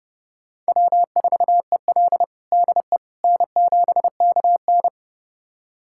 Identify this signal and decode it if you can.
Morse code, heard as a keyed tone, about 30 wpm, 700 Hz. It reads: W4EL DE N7KN
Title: USA Callsign DE Callsign - 4 Characters - 30wpm